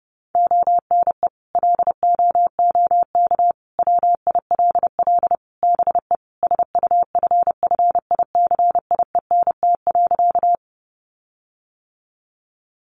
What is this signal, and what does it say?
Morse code, heard as a keyed tone, about 30 wpm, 700 Hz. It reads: ONE LOOK WILL BE SUFFICIENT.
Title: Sentences from Top 1400 Words 30wpm